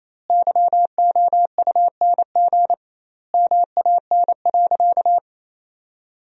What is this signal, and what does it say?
Morse code, heard as a keyed tone, about 28 wpm, 700 Hz. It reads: YOUNG MAN.
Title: 2x Word-Spacing - Sentences from Top 400 Words - 28wpm